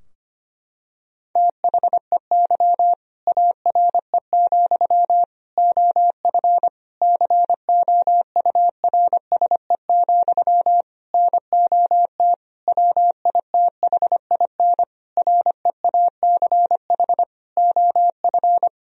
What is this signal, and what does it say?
Morse code, heard as a keyed tone, about 25 wpm, 700 Hz. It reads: THEY ARE, OF COURSE, NOT WITHIN REACH OF
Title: Sentences from Top 1000 Words - Review Part 4 - 25wpm